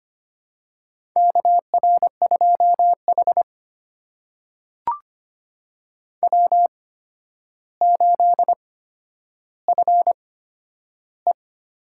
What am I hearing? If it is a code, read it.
Morse code, heard as a keyed tone, about 25 wpm, spaced as for 11 wpm, 700 Hz. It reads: KR2H W8FE
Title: USA Call Signs - 4 Characters - 8x Character Spacing - 25wpm